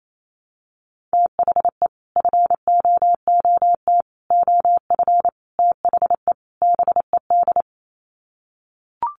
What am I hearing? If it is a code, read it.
Morse code, heard as a keyed tone, about 28 wpm, 700 Hz. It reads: THE FOOT OF THE BED
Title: Sets of 5 Words Encore2 28wpm